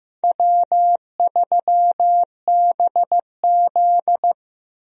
Morse code, heard as a keyed tone, about 15 wpm, 700 Hz. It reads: W3BZ
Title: USA Call Signs - 4 Characters - Repeated 3x - 15wpm